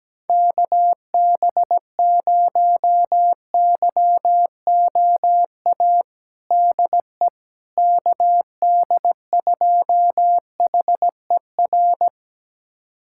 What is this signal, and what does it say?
Morse code, heard as a keyed tone, about 17 wpm, 700 Hz. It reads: KB0YOA DE KD2HER
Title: USA Callsign DE Callsign - 6 Characters - 17wpm